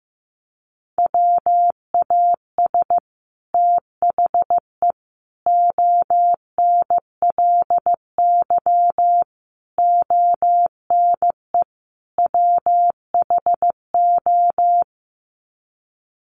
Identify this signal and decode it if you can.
Morse code, heard as a keyed tone, about 15 wpm, 700 Hz. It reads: WAS THE ONLY ONE WHO
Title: Sets of 5 Words Encore2 15wpm